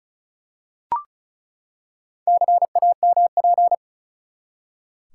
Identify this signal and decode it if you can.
Morse code, heard as a keyed tone, about 34 wpm, 700 Hz. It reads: CAMP
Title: Four Letter Words - 35wpm